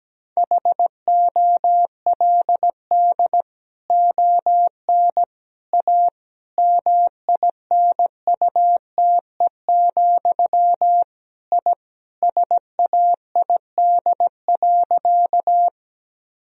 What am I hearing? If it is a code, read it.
Morse code, heard as a keyed tone, about 17 wpm, 700 Hz. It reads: HOLD ON A MINUTE, I SAID.
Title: Sentences from Top 1000 Words - Review Part 2 - 17wpm